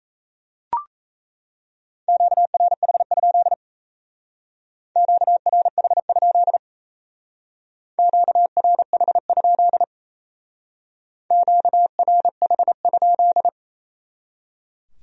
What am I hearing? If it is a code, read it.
Morse code, heard as a keyed tone, about 37 wpm, 700 Hz. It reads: QRH? QRH? QRH? QRH?